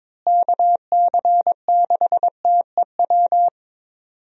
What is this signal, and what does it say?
Morse code, heard as a keyed tone, about 22 wpm, 700 Hz. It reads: KC6TEW
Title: USA Call Signs - 6 Characters - Repeated 3x - 22wpm